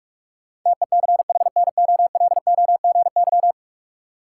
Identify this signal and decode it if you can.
Morse code, heard as a keyed tone, about 41 wpm, 700 Hz. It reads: TECHNOLOGY
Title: Sentences from Top 1000 Words - Review Part 5 - New Word Drill - 45wpm